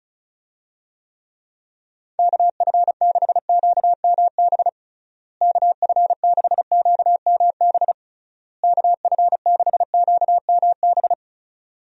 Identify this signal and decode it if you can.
Morse code, heard as a keyed tone, about 34 wpm, 700 Hz. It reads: KF6QMB KF6QMB KF6QMB
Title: USA Call Signs - 6 Characters - Repeated 3x - 35wpm